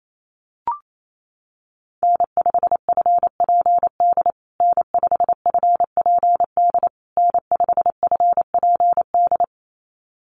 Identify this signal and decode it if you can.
Morse code, heard as a keyed tone, about 28 wpm, 690 Hz. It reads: N5FPD N5FPD N5FPD
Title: USA Call Signs Repeated 3x 28wpm